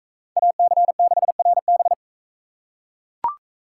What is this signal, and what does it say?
Morse code, heard as a keyed tone, about 38 wpm, 700 Hz. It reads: AC6RB